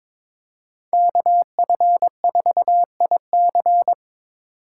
Morse code, heard as a keyed tone, about 26 wpm, 700 Hz. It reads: KF4IC
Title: Speed Racing - USA 2x2 Call Signs 22wpm